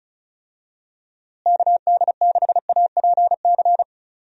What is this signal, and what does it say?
Morse code, heard as a keyed tone, about 34 wpm, 700 Hz. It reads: KD6APC
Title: USA Call Signs - 6 Characters - Repeated 2x - 35wpm